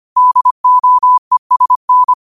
Morse code, heard as a keyed tone, about 25 wpm, 1000 Hz. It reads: DOESN
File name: qotdfiles / 25wpm